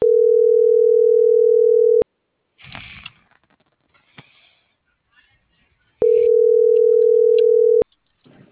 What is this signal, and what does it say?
Ringback tone (North America), 440 + 480 Hz.